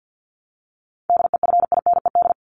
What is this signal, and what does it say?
Morse code, heard as a keyed tone, about 50 wpm, 700 Hz. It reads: DEFINED
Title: ED Suffix - Top 100 Words with Base Words 50wpm